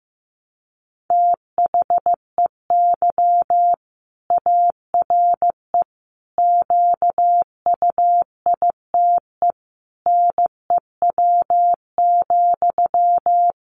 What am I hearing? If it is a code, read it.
Morse code, heard as a keyed tone, about 15 wpm, 700 Hz. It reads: THEY ARE QUITE NEW,
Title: Sentences from Top 700 Words 15wpm